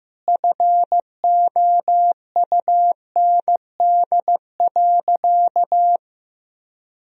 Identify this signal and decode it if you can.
Morse code, heard as a keyed tone, about 15 wpm, 700 Hz. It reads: FOUND.
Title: Sentences from Top 300 Words and Abbreviations 15wpm